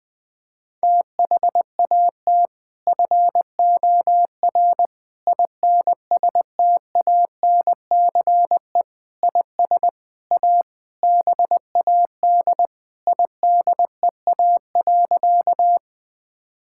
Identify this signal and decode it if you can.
Morse code, heard as a keyed tone, about 20 wpm, 700 Hz. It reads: THAT FOR INSTANCE IS A BAD IDEA.
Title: Sentences from Top 1000 Words - Review Part 5 - 20wpm